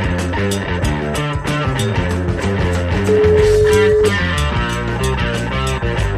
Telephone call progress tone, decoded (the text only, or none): ringback tone